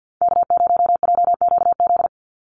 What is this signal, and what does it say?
Morse code, heard as a keyed tone, about 50 wpm, 700 Hz. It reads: K0JQZ
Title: NA SOTA Activator Callsigns 50wpm